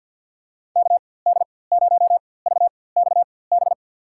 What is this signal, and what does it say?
Morse code, heard as a keyed tone, about 53 wpm, spaced as for 24 wpm, 700 Hz. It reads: KD0VXB
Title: USA Call Signs - 6 Characters - 4x Character Spacing - 50wpm